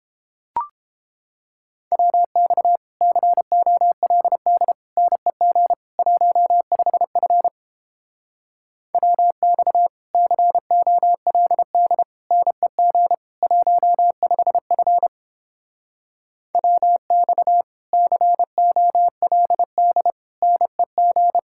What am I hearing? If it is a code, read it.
Morse code, heard as a keyed tone, about 29 wpm, 700 Hz. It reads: WX COLD NEG 15F WX COLD NEG 15F WX COLD NEG